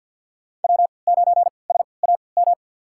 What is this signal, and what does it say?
Morse code, heard as a keyed tone, about 52 wpm, spaced as for 29 wpm, 700 Hz. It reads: W9SAM
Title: USA Call Signs - 5 Characters - 3x Character Spacing - 50wpm